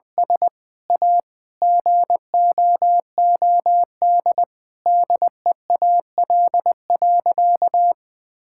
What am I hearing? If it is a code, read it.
Morse code, heard as a keyed tone, about 20 wpm, 700 Hz. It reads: S A GOOD DEAL.
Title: Sentences with 7-Word Phrases - 20wpm